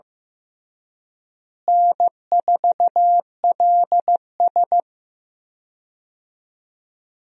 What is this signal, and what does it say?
Morse code, heard as a keyed tone, about 15 wpm, 700 Hz. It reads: N4LS
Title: USA Call Signs - 4 Characters - Repeated 3x - 15wpm